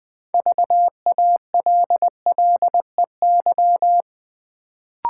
Morse code, heard as a keyed tone, about 20 wpm, 700 Hz. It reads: VALLEY
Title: Build A Word - 2 Letters At A Time - 6-Letter Words - 20wpm